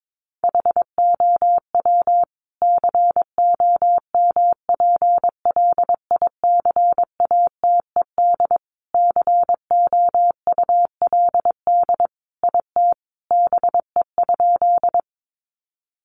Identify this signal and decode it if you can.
Morse code, heard as a keyed tone, about 22 wpm, 700 Hz. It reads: HOW COMPLICATED COULD IT BE?
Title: Sentences from Top 1200 Words 22wpm